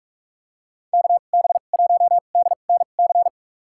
Morse code, heard as a keyed tone, about 46 wpm, spaced as for 32 wpm, 700 Hz. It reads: KB1DNC
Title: USA Call Signs - 2x Character Spacing - 45wpm